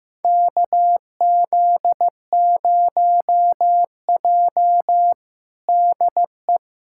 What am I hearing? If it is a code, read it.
Morse code, heard as a keyed tone, about 16 wpm, 700 Hz. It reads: KZ0J DE
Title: USA Callsign DE Callsign - 4 Characters - 15wpm